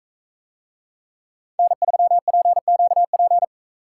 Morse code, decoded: N3PQP

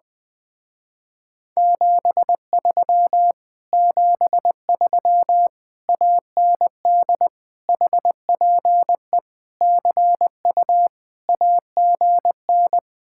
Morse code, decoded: 73 73 AND HPE CU AGN

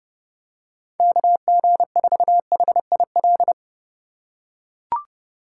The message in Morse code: KG4HIL

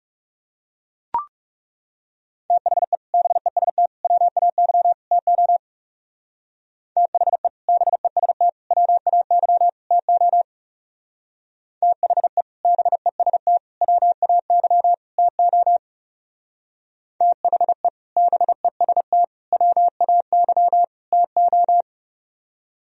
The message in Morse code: THE BEST WAY TO THE BEST WAY TO THE BEST WAY TO THE BEST WAY TO

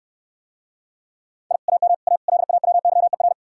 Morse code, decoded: I AM A LAWYER

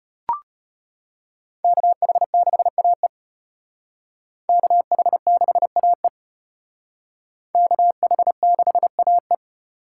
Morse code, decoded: KH6AE KH6AE KH6AE

